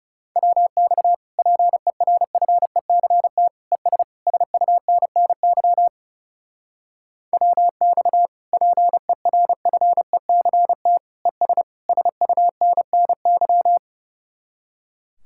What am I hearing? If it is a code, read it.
Morse code, heard as a keyed tone, about 36 wpm, 700 Hz. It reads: WX PERFECT ES SUNNY WX PERFECT ES SUNNY
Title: Speed-Racing Common QSO Phrases - 30wpm